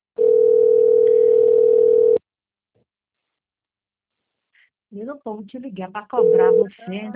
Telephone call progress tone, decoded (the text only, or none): ringback tone